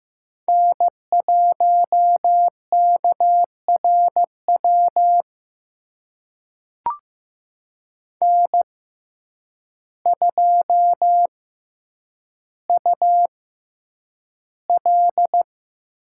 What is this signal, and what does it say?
Morse code, heard as a keyed tone, about 15 wpm, spaced as for 8 wpm, 700 Hz. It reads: N1KRW N2UL
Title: USA Call Signs - 5 Characters - 6x Character Spacing - 15wpm